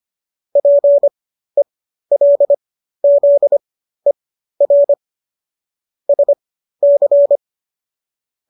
Morse code, heard as a keyed tone, about 25 wpm, spaced as for 13 wpm, 570 Hz. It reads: PELZER SC
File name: qso206 13